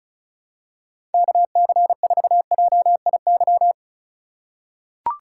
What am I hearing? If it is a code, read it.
Morse code, heard as a keyed tone, about 34 wpm, 700 Hz. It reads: KC4JIY
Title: USA Call Signs - 6 Characters - 2x Character Spacing - 35wpm